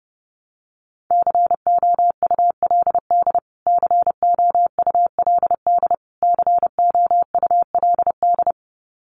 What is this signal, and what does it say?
Morse code, heard as a keyed tone, about 30 wpm, 700 Hz. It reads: COULD COULD COULD
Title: Top 200 Words Repeated 30wpm